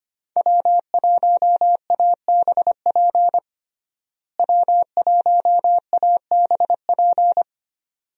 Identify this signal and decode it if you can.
Morse code, heard as a keyed tone, about 25 wpm, 700 Hz. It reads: W1ABP W1ABP
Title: USA Call Signs - 5 Characters - Repeated 2x - 25wpm